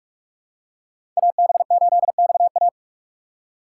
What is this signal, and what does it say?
Morse code, heard as a keyed tone, about 44 wpm, 700 Hz. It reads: AB8XA
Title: USA Call Signs - 5 Characters - 2x Character Spacing - 45wpm